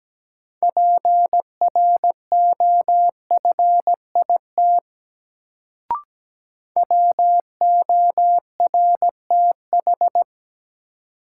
Morse code, heard as a keyed tone, about 17 wpm, 700 Hz. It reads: PROFIT WORTH
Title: Sentences from Top 800 Words - New Word Drill - 17wpm